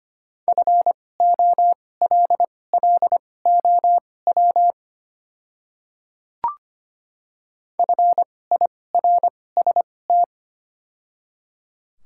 Morse code, heard as a keyed tone, about 25 wpm, spaced as for 18 wpm, 700 Hz. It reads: FOLLOW FIRST
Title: 2x Character Spacing - Top 200 Words - 25wpm